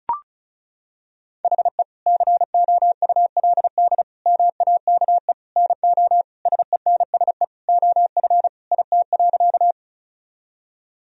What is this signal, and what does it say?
Morse code, heard as a keyed tone, about 34 wpm, 700 Hz. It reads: HE COULD MAKE NO SENSE OF IT.
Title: Sentences from Top 1000 Words - Review Part 4 - 35wpm